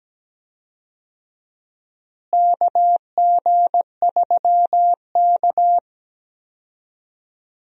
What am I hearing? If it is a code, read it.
Morse code, heard as a keyed tone, about 17 wpm, 700 Hz. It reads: KG3K
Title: USA Call Signs - 4 Characters - Repeated 2x - 17wpm